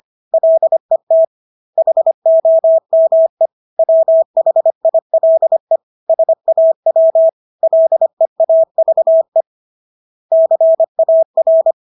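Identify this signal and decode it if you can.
Morse code, heard as a keyed tone, about 25 wpm, 650 Hz. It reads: LET HOME WHILE SAW LEAVE CAR
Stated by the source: Morse practice of English words at 25 wpm from CW med Gnister.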